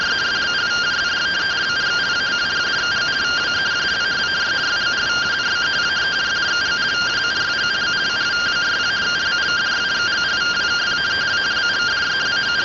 Two-tone teleprinter signal, about 170 Hz shift.